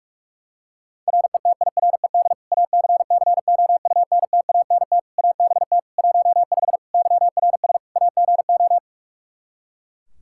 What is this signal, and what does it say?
Morse code, heard as a keyed tone, about 42 wpm, 700 Hz. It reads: RETIRED ACCOUNTANT ABT 15 YRS AGO
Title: Practice QSO - Exchange 3 - More Info - 45wpm